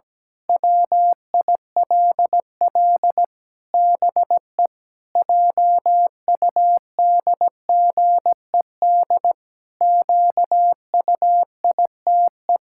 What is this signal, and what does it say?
Morse code, heard as a keyed tone, about 17 wpm, 700 Hz. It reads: WILL BE JUDGED QUITE
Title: Pangram Challenge v2 - 17wpm